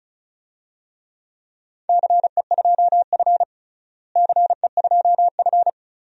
Morse code, heard as a keyed tone, about 36 wpm, 700 Hz. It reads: CE2F CE2F
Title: DXCC-100 Call Signs - 4 Characters - Repeated 3x - 35wpm